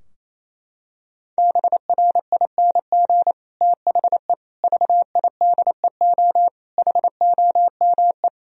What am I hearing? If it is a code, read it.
Morse code, heard as a keyed tone, about 27 wpm, 700 Hz. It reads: BRING THE VIDEO HOME
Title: Sentences from Top 1000 Words - Review Part 4 - 28wpm